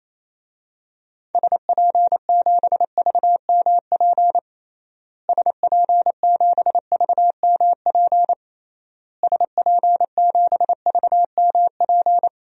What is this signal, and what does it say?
Morse code, heard as a keyed tone, about 28 wpm, 700 Hz. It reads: SP7VMP SP7VMP SP7VMP
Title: DXCC-100 Call Signs - 6 Characters - Repeated 3x - 28wpm